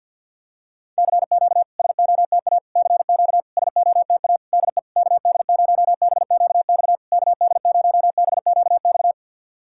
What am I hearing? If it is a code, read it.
Morse code, heard as a keyed tone, about 51 wpm, 690 Hz. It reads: CQ SOTA CQ SOTA DE KD0BQX KD0BQX
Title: SOTA QSO Practice 50wpm